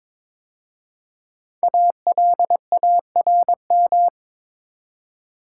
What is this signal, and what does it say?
Morse code, heard as a keyed tone, about 22 wpm, 700 Hz. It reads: ALARM
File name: 5 Letter Words - First 3 Letters Given - 22wpm